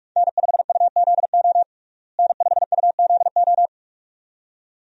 Morse code, heard as a keyed tone, about 47 wpm, 700 Hz. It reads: N5UZO N5UZO
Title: USA Call Signs - 5 Characters - Repeated 2x - 45wpm